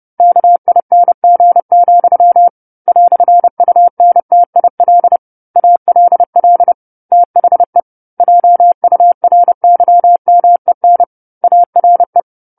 Morse code, heard as a keyed tone, about 30 wpm, 700 Hz. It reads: KING, "UNTIL ALL THE JURYMEN ARE
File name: 10-Chapter-10 30 WPM